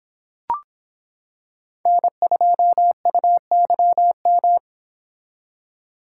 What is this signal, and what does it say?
Morse code, heard as a keyed tone, about 26 wpm, 700 Hz. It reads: N2UYM